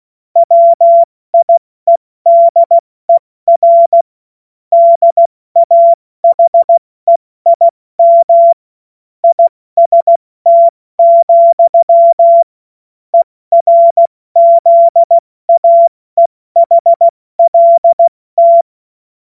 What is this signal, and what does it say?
Morse code, heard as a keyed tone, about 16 wpm, 680 Hz. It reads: WIEDER DAHEIM IST, ERZAEHLT
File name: Morsekurs / Witze / Witz 19. Oktober 2021, 70 F80